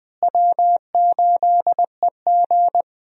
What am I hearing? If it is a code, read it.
Morse code, heard as a keyed tone, about 20 wpm, 700 Hz. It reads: W8EG
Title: USA Call Signs - 4 Characters - Repeated 3x - 20wpm